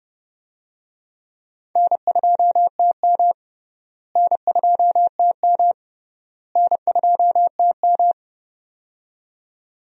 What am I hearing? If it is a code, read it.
Morse code, heard as a keyed tone, about 30 wpm, 700 Hz. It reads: N2TM N2TM N2TM
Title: USA Call Signs - 4 Characters - Repeated 3x - 30wpm